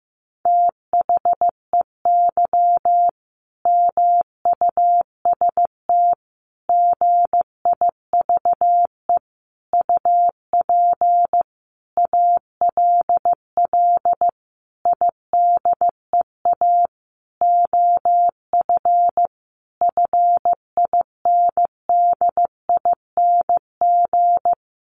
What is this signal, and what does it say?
Morse code, heard as a keyed tone, about 15 wpm, 700 Hz. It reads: THEY MUST GIVE UP ALL IDEA OF FINDING
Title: Sentences from Top 1100 Words 15wpm